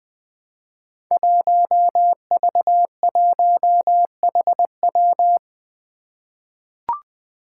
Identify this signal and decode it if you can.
Morse code, heard as a keyed tone, about 20 wpm, 700 Hz. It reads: 1V1HW
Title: Five Challenging Letter-Number Combinations 20wpm